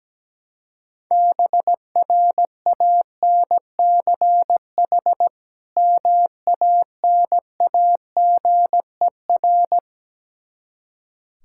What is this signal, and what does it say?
Morse code, heard as a keyed tone, about 17 wpm, 700 Hz. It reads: BRANCH MANAGER
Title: Practice QSO - Exchange 3 - More Info - 17wpm